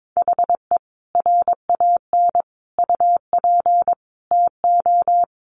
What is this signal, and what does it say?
Morse code, heard as a keyed tone, about 22 wpm, 700 Hz. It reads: HE RAN UP TO
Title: Sentences from Top 500 Words 22wpm